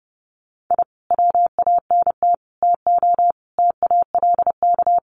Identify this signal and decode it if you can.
Morse code, heard as a keyed tone, about 30 wpm, 700 Hz. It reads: I WANT TO TALK
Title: Sets of 5 Words Encore 30wpm